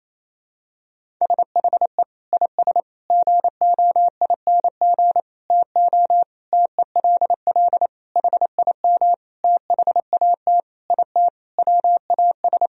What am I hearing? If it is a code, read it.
Morse code, heard as a keyed tone, about 27 wpm, 700 Hz. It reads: SHE IS GOING TO TELL HIM THAT IT WAS
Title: Sentences from Top 1000 Words - Review Part 1 - 28wpm